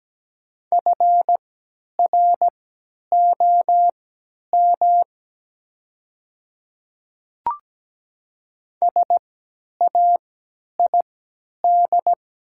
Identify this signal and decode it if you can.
Morse code, heard as a keyed tone, about 17 wpm, spaced as for 10 wpm, 700 Hz. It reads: FROM SAID
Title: 3x Character Spacing - Top 200 Words - 17wpm